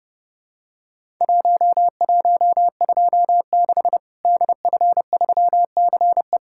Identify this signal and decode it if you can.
Morse code, heard as a keyed tone, about 29 wpm, 700 Hz. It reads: 1126 DF3CE
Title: Worked All Europe DX Contest - QTC Exchange - 30wpm